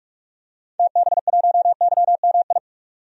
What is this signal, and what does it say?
Morse code, heard as a keyed tone, about 46 wpm, 700 Hz. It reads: TB1YMI
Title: DXCC-100 Call Signs - 6 Characters - Repeated 2x - 45wpm